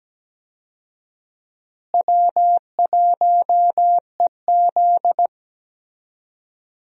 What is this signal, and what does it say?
Morse code, heard as a keyed tone, about 17 wpm, 700 Hz. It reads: W1EZ
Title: USA Call Signs - 4 Characters - Repeated 2x - 17wpm